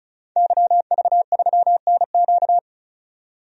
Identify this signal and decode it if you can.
Morse code, heard as a keyed tone, about 34 wpm, 700 Hz. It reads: YV3NQ
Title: DXCC-100 Call Signs - 5 Characters - 35wpm